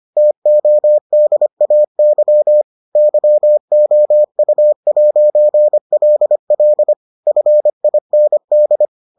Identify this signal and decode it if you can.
Morse code, heard as a keyed tone, about 25 wpm, 600 Hz. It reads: TODAY YOU'LL FIND